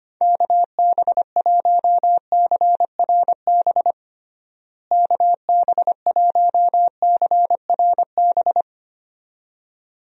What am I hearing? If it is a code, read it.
Morse code, heard as a keyed tone, about 25 wpm, 700 Hz. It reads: KB1CRB KB1CRB
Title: USA Call Signs - 6 Characters - Repeated 2x - 25wpm